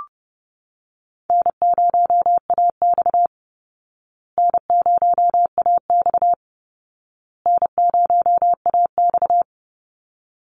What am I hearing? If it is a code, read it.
Morse code, heard as a keyed tone, about 30 wpm, 700 Hz. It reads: N0AX N0AX N0AX
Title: Call Signs Repeated 2x - 30wpm